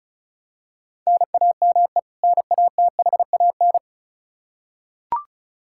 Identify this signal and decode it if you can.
Morse code, heard as a keyed tone, about 36 wpm, 700 Hz. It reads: NAME NATHAN
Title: Practice QSO - Exchange 1 - Beginning - 35wpm